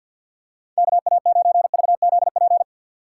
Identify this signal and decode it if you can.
Morse code, heard as a keyed tone, about 53 wpm, 700 Hz. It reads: KA9VZP
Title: USA Call Signs - 6 Characters - 6x Character Spacing - 50wpm